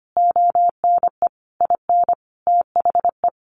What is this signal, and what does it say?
Morse code, heard as a keyed tone, about 25 wpm, 700 Hz. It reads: ONE IN THE
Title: Sentences from Top 1300 Words 25wpm